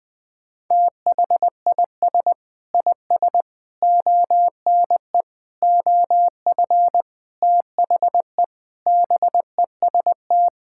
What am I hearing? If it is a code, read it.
Morse code, heard as a keyed tone, about 20 wpm, 700 Hz. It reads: THIS IS ONE OF THE BEST
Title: Sentences with 5-Word Phrases - 20wpm